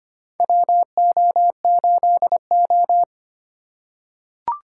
Morse code, heard as a keyed tone, about 25 wpm, 700 Hz. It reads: WO8O
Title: USA Call Signs - 4 Characters - 2x Character Spacing - 25wpm